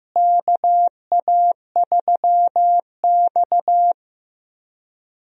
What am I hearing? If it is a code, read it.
Morse code, heard as a keyed tone, about 15 wpm, 700 Hz. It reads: KA3X
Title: USA Call Signs - 4 Characters - Repeated 3x - 15wpm